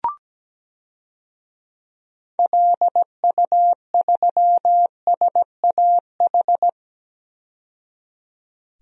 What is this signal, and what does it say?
Morse code, heard as a keyed tone, about 17 wpm, 700 Hz. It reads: LU3SAH
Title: DXCC-100 Call Signs - 6 Characters - 17wpm